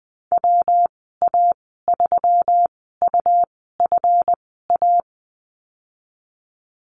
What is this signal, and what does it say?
Morse code, heard as a keyed tone, about 20 wpm, spaced as for 15 wpm, 700 Hz. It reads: WA3UFA
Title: 167 Lesson 27g - Number 7 - Call Signs 20wpm